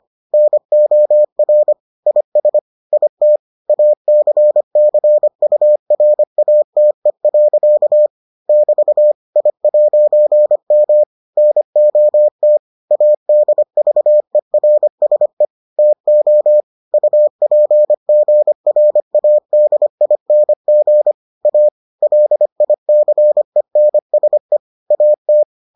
Morse code, heard as a keyed tone, about 25 wpm, 600 Hz. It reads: NOR IS IT ACCURATE. <BT> I'M NOT ADVERSE TO UPGRADING A LICENSE AT